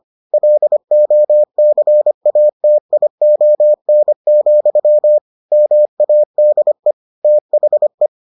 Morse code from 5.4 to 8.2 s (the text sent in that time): MADE THE